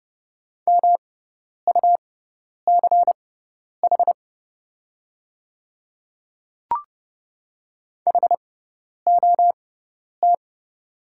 Morse code, heard as a keyed tone, about 30 wpm, spaced as for 10 wpm, 700 Hz. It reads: MUCH HOT
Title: 6x Character Spacing - Top 200 Words - 30wpm